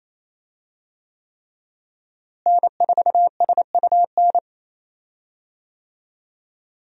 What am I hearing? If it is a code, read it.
Morse code, heard as a keyed tone, about 28 wpm, 700 Hz. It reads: N4SUN
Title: USA Call Signs - 5 Characters - Repeated 3x - 28wpm